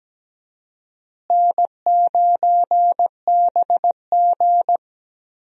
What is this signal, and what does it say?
Morse code, heard as a keyed tone, about 17 wpm, 700 Hz. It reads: N9BG
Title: USA Call Signs - 4 Characters - Repeated 3x - 17wpm